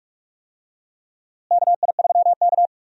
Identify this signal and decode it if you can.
Morse code, heard as a keyed tone, about 46 wpm, 700 Hz. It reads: KI3K
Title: USA Call Signs - 4 Characters - Repeated 3x - 45wpm